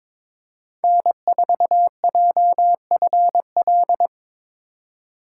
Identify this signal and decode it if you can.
Morse code, heard as a keyed tone, about 22 wpm, 700 Hz. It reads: N4JFL
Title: USA Call Signs - 5 Characters - 3x Character Spacing - 22wpm